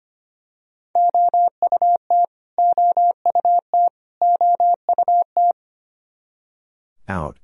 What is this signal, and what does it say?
Morse code, heard as a keyed tone, about 25 wpm, 700 Hz. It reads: OUT OUT OUT
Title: Top 100 Words Repeated 25wpm